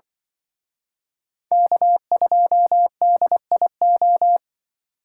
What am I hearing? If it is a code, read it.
Morse code, heard as a keyed tone, about 24 wpm, 700 Hz. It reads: K2DIO
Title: Speed Racing - USA 1x3 Call Signs 20wpm